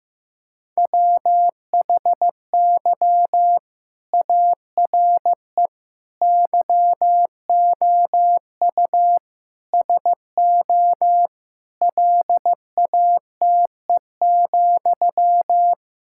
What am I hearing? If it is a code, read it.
Morse code, heard as a keyed tone, about 16 wpm, 700 Hz. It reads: WHY ARE YOU SO LATE,
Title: Sentences from Top 1000 Words - Review Part 6 - 15wpm